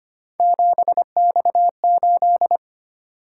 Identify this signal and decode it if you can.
Morse code, heard as a keyed tone, about 25 wpm, 700 Hz. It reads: 7X8
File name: Three Letter-Number Combinations - 25wpm